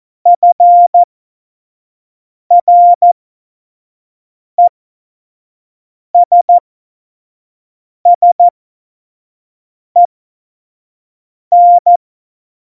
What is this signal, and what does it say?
Morse code, heard as a keyed tone, about 14 wpm, spaced as for 5 wpm, 700 Hz. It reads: FRESSEN
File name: Morsekurs / Witze / Witz 09. Oktober 2018, 25 F70